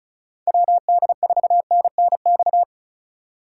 Morse code, heard as a keyed tone, about 36 wpm, 700 Hz. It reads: WD4NNX
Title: USA Call Signs - 4x Character Spacing - 35wpm